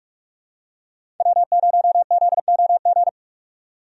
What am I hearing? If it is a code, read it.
Morse code, heard as a keyed tone, about 45 wpm, 700 Hz. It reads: W0ZOG
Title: USA Call Signs - 5 Characters - 45wpm